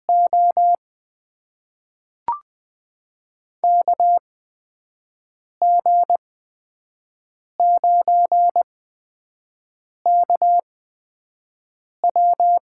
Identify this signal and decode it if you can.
Morse code, heard as a keyed tone, about 20 wpm, spaced as for 5 wpm, 700 Hz. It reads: O KG9KW
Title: USA Call Signs - 8x Character Spacing - 20wpm